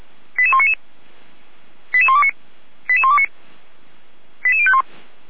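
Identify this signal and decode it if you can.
Five-tone selective call (ZVEI), heard as quick runs of five tones, tones of about 70 ms. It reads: 88199 88129 88129 88051
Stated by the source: ZVEI five-tone call 88128 received by SDR, logged on the Feuerwehr Orlamünde server.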